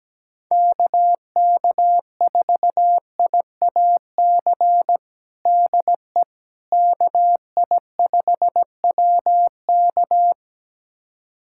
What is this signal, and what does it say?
Morse code, heard as a keyed tone, about 17 wpm, 700 Hz. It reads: KK4IAC DE KI5WK
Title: USA Callsign DE Callsign 17wpm